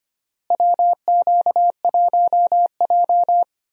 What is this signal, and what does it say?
Morse code, heard as a keyed tone, about 25 wpm, 700 Hz. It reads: WQ1J
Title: USA Call Signs - 4 Characters - Repeated 3x - 25wpm